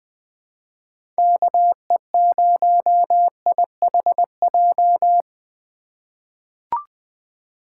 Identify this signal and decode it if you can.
Morse code, heard as a keyed tone, about 20 wpm, 700 Hz. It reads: KE0IHJ
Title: USA Call Signs - 6 Characters - 4x Character Spacing - 20wpm